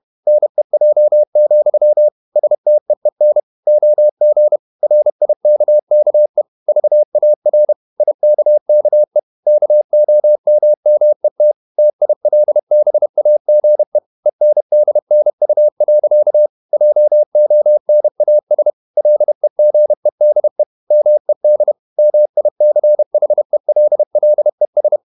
Morse code, heard as a keyed tone, about 31 wpm, 600 Hz. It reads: NEJ, STEEN OG RIKKE VAR IKKE KOMMET TILBAGE ENDNU. JONAS LEGEDE MED MICHELLES
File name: Ferie QRQ 31wpm